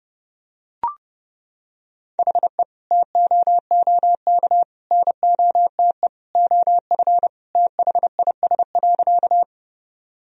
Morse code, heard as a keyed tone, about 32 wpm, 700 Hz. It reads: HE TOOK NOTE OF THIS.
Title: Sentences from Top 1000 Words - Review Part 2 - 30wpm